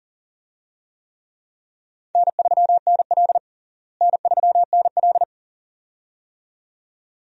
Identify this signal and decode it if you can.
Morse code, heard as a keyed tone, about 40 wpm, 700 Hz. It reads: N3NL N3NL
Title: USA Call Signs - 4 Characters - Repeated 2x - 40wpm